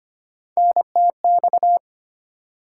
Morse code, heard as a keyed tone, about 25 wpm, 700 Hz. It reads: NTX
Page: morse-code-ninja-practice-25wpm